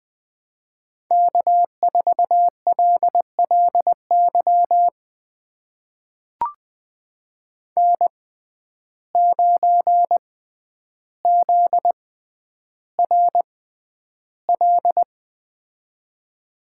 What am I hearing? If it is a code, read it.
Morse code, heard as a keyed tone, about 20 wpm, spaced as for 10 wpm, 700 Hz. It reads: K4LLY N9ZRL
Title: USA Call Signs - 6x Character Spacing - 20wpm